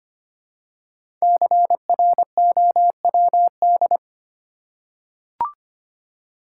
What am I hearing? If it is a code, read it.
Morse code, heard as a keyed tone, about 25 wpm, 700 Hz. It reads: CROWD
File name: 5 Letter Words - First 3 Letters Given - 25wpm